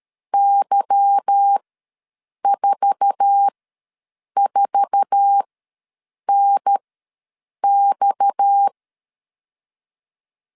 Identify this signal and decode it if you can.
Morse code, heard as a keyed tone, about 13 wpm, spaced as for 7 wpm, 800 Hz. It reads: Y44NX
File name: LEZIONI / 8 YJ45 / 8 YJ4 ws 12wpm 3249